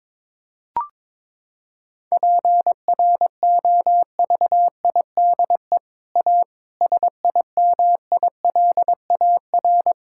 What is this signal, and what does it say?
Morse code, heard as a keyed tone, about 21 wpm, 700 Hz. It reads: PROVIDE A SIMILAR
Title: Sentences from Top 1000 Words - Review Part 5 - 22wpm